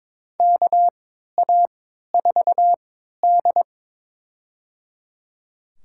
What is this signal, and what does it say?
Morse code, heard as a keyed tone, about 22 wpm, spaced as for 13 wpm, 700 Hz. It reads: KA4D
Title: USA Call Signs - 3x Character Spacing - 22wpm